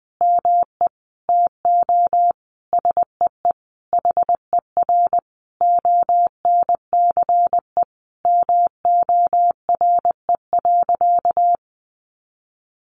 Morse code, heard as a keyed tone, about 20 wpm, 700 Hz. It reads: ME TO SEE HER ONCE MORE.
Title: Sentences from Top 500 Words 20wpm